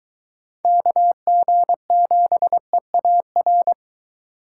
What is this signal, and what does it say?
Morse code, heard as a keyed tone, about 22 wpm, 700 Hz. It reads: KG7EAR